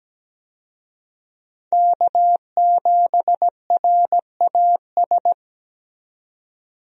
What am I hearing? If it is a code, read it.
Morse code, heard as a keyed tone, about 17 wpm, 700 Hz. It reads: K7RAS
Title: USA Call Signs - 5 Characters - 17wpm